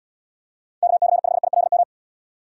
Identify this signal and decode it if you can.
Morse code, heard as a keyed tone, about 70 wpm, 700 Hz. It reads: KC5ILM